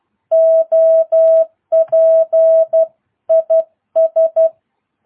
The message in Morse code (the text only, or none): OP IS